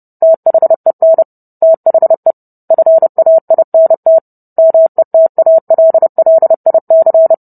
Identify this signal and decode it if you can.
Morse code, heard as a keyed tone, about 30 wpm, 650 Hz. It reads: THEN THE FAINT METALLIC